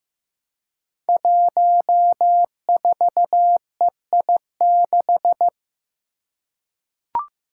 Morse code, heard as a keyed tone, about 15 wpm, 700 Hz. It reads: 14EI6
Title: Five Challenging Letter-Number Combinations 15wpm